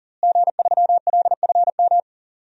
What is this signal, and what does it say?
Morse code, heard as a keyed tone, about 40 wpm, 700 Hz. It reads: G3LFM
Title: DXCC-100 Call Signs - 5 Characters - Repeated 3x - 40wpm